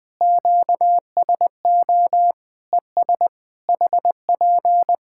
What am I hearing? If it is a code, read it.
Morse code, heard as a keyed tone, about 20 wpm, 700 Hz. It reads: QSO ES HP
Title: Practice QSO - Exchange 4 - Ending - 20wpm